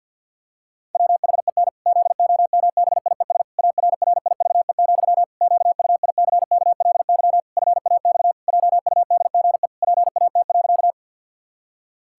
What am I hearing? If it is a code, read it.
Morse code, heard as a keyed tone, about 51 wpm, 700 Hz. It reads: WHEN ZOMBIES ARRIVE, QUICKLY FAX JUDGE PAT.
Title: Pangram Challenge v2 - 50wpm